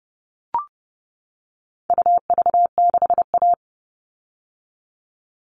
Morse code, heard as a keyed tone, about 30 wpm, 700 Hz. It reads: UV6A
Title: All Cut Numbers - 4-Digit 30wpm